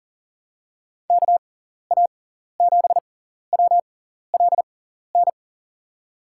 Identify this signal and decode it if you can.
Morse code, heard as a keyed tone, about 40 wpm, spaced as for 14 wpm, 700 Hz. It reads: KA7WLN
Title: USA Call Signs - 6 Characters - 6x Character Spacing - 40wpm